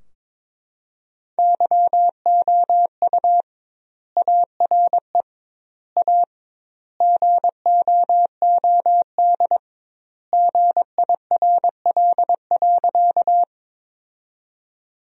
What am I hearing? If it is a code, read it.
Morse code, heard as a keyed tone, about 22 wpm, 700 Hz. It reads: YOU ARE A GOOD GIRL.
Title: 2x Word-Spacing - Sentences from Top 400 Words - 22wpm